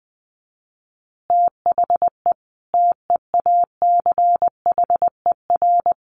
Morse code, heard as a keyed tone, about 20 wpm, 700 Hz. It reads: THE TEACHER
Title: Additional Easy Phrases 20wpm